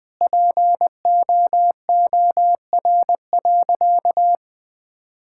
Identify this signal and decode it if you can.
Morse code, heard as a keyed tone, about 20 wpm, 700 Hz. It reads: POOR.
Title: Sentences from Top 1000 Words - Review Part 6 - 20wpm